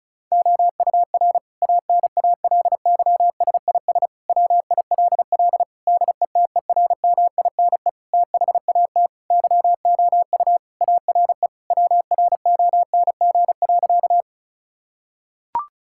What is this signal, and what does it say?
Morse code, heard as a keyed tone, about 34 wpm, 700 Hz. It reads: OUR ANALYSIS WILL DETERMINE THAT YOU ARE WRONG.
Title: Sentences from Top 1000 Words - Review Part 5 - 35wpm